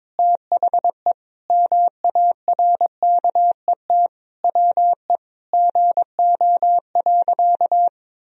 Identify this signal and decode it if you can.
Morse code, heard as a keyed tone, about 22 wpm, 700 Hz. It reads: THE MARKET WE GO.
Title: Sentences from Top 1000 Words - Review Part 3 - 22wpm